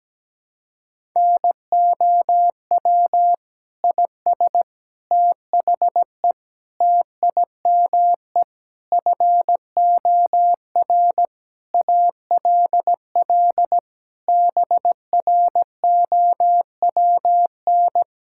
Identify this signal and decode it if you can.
Morse code, heard as a keyed tone, about 17 wpm, 700 Hz. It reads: NOW IS THE TIME FOR ALL BROWN
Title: Pangram Challenge v2 - 17wpm